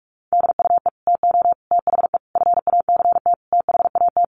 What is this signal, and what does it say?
Morse code, heard as a keyed tone, about 44 wpm, 700 Hz. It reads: DUE TO THE FACT THAT
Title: Sets of 5 Words Encore 45wpm